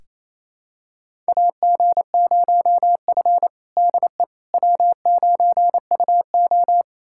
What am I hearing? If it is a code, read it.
Morse code, heard as a keyed tone, about 28 wpm, 700 Hz. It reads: AG0F DE W9UO
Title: USA Callsign DE Callsign - 4 Characters - 28wpm